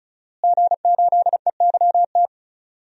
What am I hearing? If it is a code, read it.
Morse code, heard as a keyed tone, about 36 wpm, 700 Hz. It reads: G8EYT
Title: DXCC-100 Call Signs - 5 Characters - Repeated 3x - 35wpm